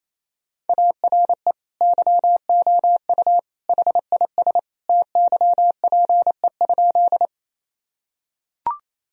Morse code, heard as a keyed tone, about 28 wpm, 700 Hz. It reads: ARE YOU HIS TYPE?
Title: Sentences from Top 1000 Words - Review Part 3 - 28wpm